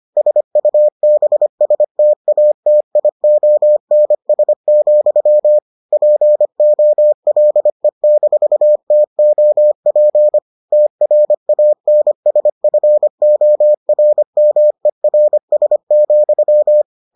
Morse code, heard as a keyed tone, about 25 wpm, 600 Hz. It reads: SUBSTATIONS, POLE-TOP TRANSFORMERS,